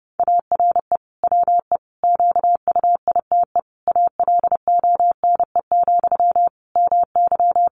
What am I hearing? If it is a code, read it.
Morse code, heard as a keyed tone, about 30 wpm, 700 Hz. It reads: ARE WE QUITE ALONE, MY
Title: Sentences from Top 1000 Words 30wpm